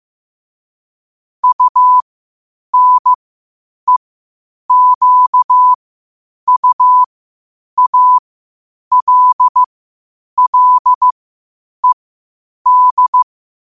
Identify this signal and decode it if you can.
Morse code, heard as a keyed tone, about 15 wpm, spaced as for 8 wpm, 1000 Hz. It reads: UNEQUALLED
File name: qotdfiles / 8wpm